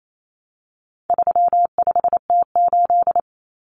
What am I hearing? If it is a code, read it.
Morse code, heard as a keyed tone, about 28 wpm, 700 Hz. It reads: 35T8
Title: Common Cut Numbers - 4-Digit 28wpm